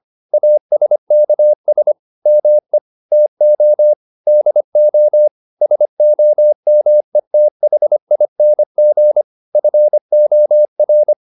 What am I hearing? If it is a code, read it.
Morse code, heard as a keyed tone, about 25 wpm, 600 Hz. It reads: ASKS ME TO DO SOMETHING FOR